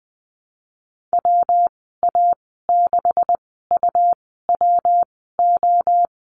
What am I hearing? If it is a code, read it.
Morse code, heard as a keyed tone, about 19 wpm, spaced as for 14 wpm, 700 Hz. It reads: WA6UWO
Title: 128 Lesson 21g - Number 6 - Call Signs 20wpm